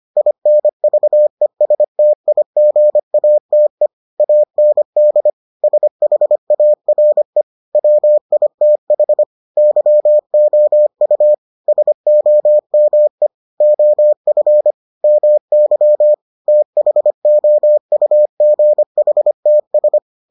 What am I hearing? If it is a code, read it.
Morse code, heard as a keyed tone, about 25 wpm, 600 Hz. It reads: INVESTIGATE AND SHARE WITH YOU SOME OF MY THOUGHTS